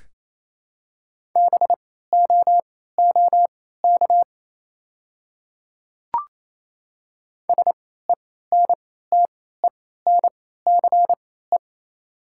Morse code, heard as a keyed tone, about 28 wpm, spaced as for 16 wpm, 700 Hz. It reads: BOOK SENTENCE
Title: 3x Character Spacing - Top 300 Words - 28wpm